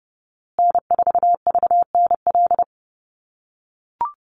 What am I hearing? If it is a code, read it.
Morse code, heard as a keyed tone, about 30 wpm, 700 Hz. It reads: N4VNL
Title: USA Call Signs Repeated 3x 30wpm